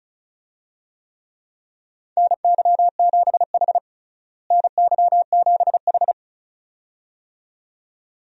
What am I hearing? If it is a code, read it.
Morse code, heard as a keyed tone, about 36 wpm, 700 Hz. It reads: NY7H NY7H
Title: USA Call Signs - 4 Characters - Repeated 2x - 35wpm